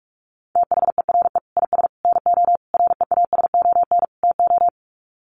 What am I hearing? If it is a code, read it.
Morse code, heard as a keyed tone, about 44 wpm, 700 Hz. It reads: THERE IS NO REASON TO
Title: Sets of 5 Words Encore 45wpm